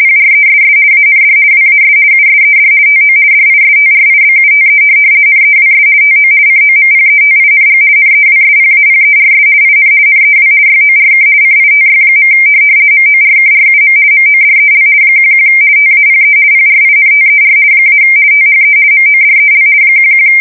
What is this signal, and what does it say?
RTTY, about 50 baud. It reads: PRESS AND WAS USED TO DISTRIBUTE NEWS TO COMPETING NEWSPAPERS IN NEW YORK CITY. MORKRUM MODEL 11 TAPE PRINTER - THE MODEL 11
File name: fsk170-50B / teletype-corp